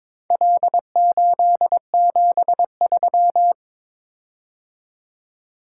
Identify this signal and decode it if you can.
Morse code, heard as a keyed tone, about 22 wpm, 700 Hz. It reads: L873
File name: practice / 22wpm / Four Letter-Number Combinations - 22wpm